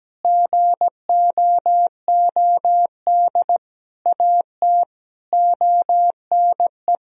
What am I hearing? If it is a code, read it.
Morse code, heard as a keyed tone, about 17 wpm, 700 Hz. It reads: GOOD AT ONE
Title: Sentences from Top 1000 Words - Review Part 2 - 17wpm